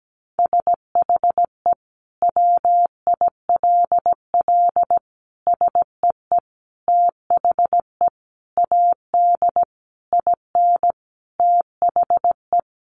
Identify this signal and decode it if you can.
Morse code, heard as a keyed tone, about 17 wpm, 700 Hz. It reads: SHE WILL SEE THE AD IN THE
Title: Sentences from Top 1300 Words 17wpm